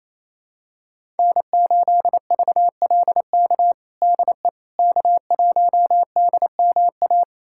Morse code, heard as a keyed tone, about 28 wpm, 700 Hz. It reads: N8VLK DE K1DMA
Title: USA Callsign DE Callsign - 5 Characters - 28wpm